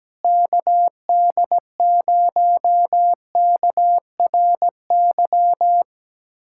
Morse code, heard as a keyed tone, about 17 wpm, 700 Hz. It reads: KD0KRY
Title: USA Callsign DE Callsign - 6 Characters - 17wpm